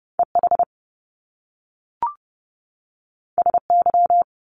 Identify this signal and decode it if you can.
Morse code, heard as a keyed tone, about 30 wpm, 700 Hz. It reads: EH SY
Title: Top Two Letter Combinations in Words 30wpm